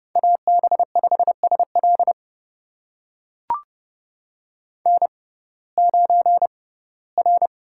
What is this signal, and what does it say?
Morse code, heard as a keyed tone, about 29 wpm, spaced as for 18 wpm, 700 Hz. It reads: AB5SL N9R
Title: USA Call Signs - 5 Characters - 6x Character Spacing - 30wpm